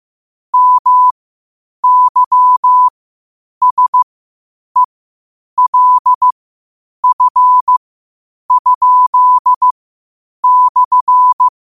Morse code, heard as a keyed tone, about 15 wpm, spaced as for 8 wpm, 1000 Hz. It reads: MYSELF?/
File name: qotdfiles / 8wpm